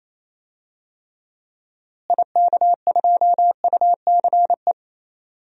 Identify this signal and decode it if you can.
Morse code, heard as a keyed tone, about 28 wpm, 700 Hz. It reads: IK2UCE
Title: DXCC-100 Call Signs - 6 Characters - Repeated 3x - 28wpm